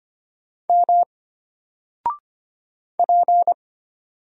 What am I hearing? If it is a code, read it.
Morse code, heard as a keyed tone, about 25 wpm, 700 Hz. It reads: M P
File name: Single-Letter-Number-25wpm